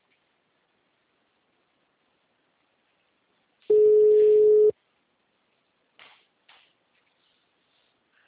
Ringback tone (Europe, 425 Hz).